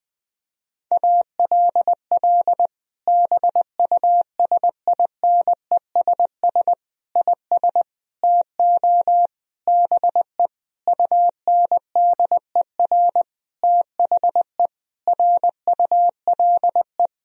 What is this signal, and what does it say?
Morse code, heard as a keyed tone, about 20 wpm, 700 Hz. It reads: ALL BUSINESS IS TO BE UNDER THE RULE
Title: Sentences from Top 1000 Words - Review Part 3 - 20wpm